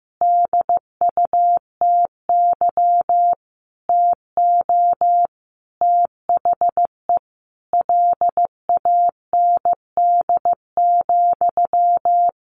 Morse code, heard as a keyed tone, about 14 wpm, 700 Hz. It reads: DUTY TO THE LAND,
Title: Sentences from Top 1500 Words 15wpm